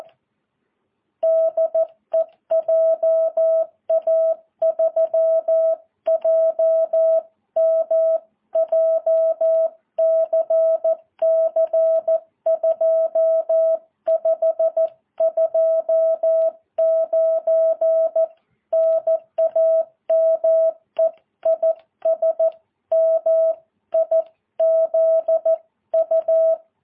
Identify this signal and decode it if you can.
Morse code, heard as a keyed tone, about 14 wpm, 650 Hz. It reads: DE JA3JM JCC 2529 NAME IS MIZU
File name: jh8cbh